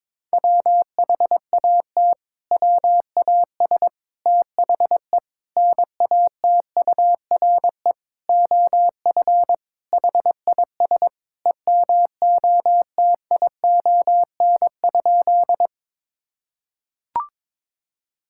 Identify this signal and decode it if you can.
Morse code, heard as a keyed tone, about 22 wpm, 700 Hz. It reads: WHAT WAS THE NATURE OF HIS EMOTION?
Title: Sentences from Top 1600 Words 22wpm